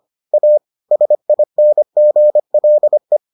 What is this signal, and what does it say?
Morse code, heard as a keyed tone, about 25 wpm, 600 Hz. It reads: A SINGLE